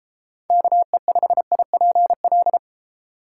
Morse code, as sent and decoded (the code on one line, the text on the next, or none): -.- . ..... .. .--. .-..
KE5IPL